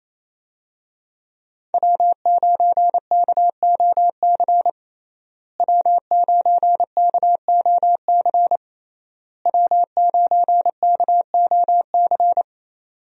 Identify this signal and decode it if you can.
Morse code, heard as a keyed tone, about 28 wpm, 700 Hz. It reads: W9KOC W9KOC W9KOC
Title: USA Call Signs - 5 Characters - Repeated 3x - 28wpm